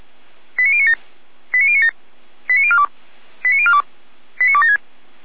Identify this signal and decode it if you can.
Five-tone selective call (ZVEI), heard as quick runs of five tones, tones of about 70 ms. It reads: 89097 89097 89042 89042 89276
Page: Server at audio.f85360.feuerwehr-orlamuende.org